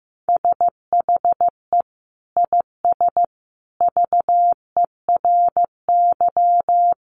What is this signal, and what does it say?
Morse code, heard as a keyed tone, about 15 wpm, 700 Hz. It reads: SHE IS VERY
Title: Sentences from Top 1400 Words 15wpm